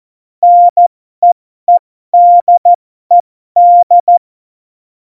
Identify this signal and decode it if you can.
Morse code, heard as a keyed tone, about 14 wpm, spaced as for 12 wpm, 700 Hz. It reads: NEEDED